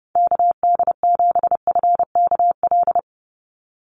Morse code, heard as a keyed tone, about 30 wpm, 700 Hz. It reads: KD7FKL
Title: USA Call Signs 30wpm